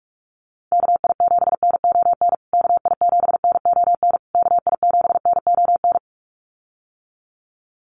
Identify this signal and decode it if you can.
Morse code, heard as a keyed tone, about 44 wpm, 700 Hz. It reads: KI7NON KI7NON KI7NON
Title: USA Call Signs Repeated 3x 45wpm